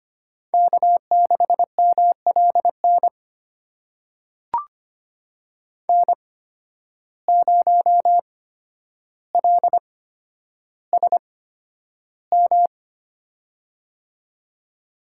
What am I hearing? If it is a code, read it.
Morse code, heard as a keyed tone, about 25 wpm, spaced as for 11 wpm, 700 Hz. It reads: K6MLN N0LSM
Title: USA Call Signs - 5 Characters - 8x Character Spacing - 25wpm